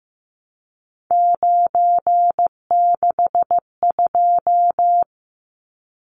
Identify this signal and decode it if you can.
Morse code, heard as a keyed tone, about 15 wpm, 700 Hz. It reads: 962
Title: Numbers - 3-Digit 15wpm